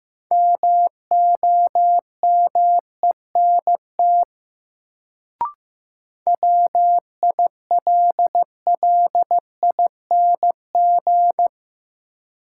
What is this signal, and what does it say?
Morse code, heard as a keyed tone, about 15 wpm, 700 Hz. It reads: MOMENT WILLING
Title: Sentences from Top 1000 Words - New Word Drill - 15wpm